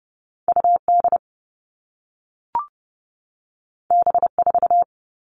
Morse code, heard as a keyed tone, about 30 wpm, 700 Hz. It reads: U D B 4
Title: All Cut Numbers - 2-Digit 30wpm